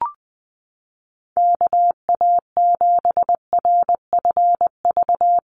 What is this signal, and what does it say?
Morse code, heard as a keyed tone, about 20 wpm, 700 Hz. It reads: KA7RFV
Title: USA Call Signs Repeated 2x 20wpm